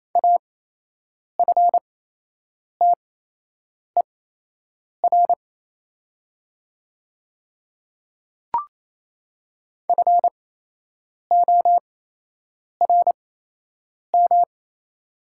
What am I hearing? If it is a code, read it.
Morse code, heard as a keyed tone, about 28 wpm, spaced as for 8 wpm, 700 Hz. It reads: AFTER FORM
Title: 8x Character Spacing - Top 300 Words - 28wpm